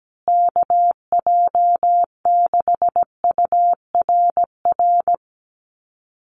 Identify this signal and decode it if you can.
Morse code, heard as a keyed tone, about 17 wpm, 700 Hz. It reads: KJ6URR
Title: USA Call Signs Repeated 2x 17wpm